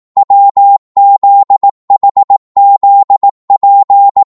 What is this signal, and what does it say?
Morse code, heard as a keyed tone, about 18 wpm, 800 Hz. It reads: WZHZP
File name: Morse / ltr-18wpm